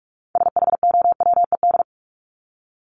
Morse code, heard as a keyed tone, about 44 wpm, 700 Hz. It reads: SHOWED
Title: ED Suffix - Top 100 Words with Base Words 45wpm